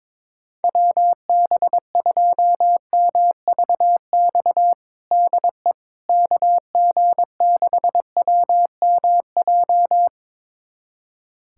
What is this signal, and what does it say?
Morse code, heard as a keyed tone, about 22 wpm, 700 Hz. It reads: WB2MVX DE KG6WMJ
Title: USA Callsign DE Callsign - 6 Characters - 22wpm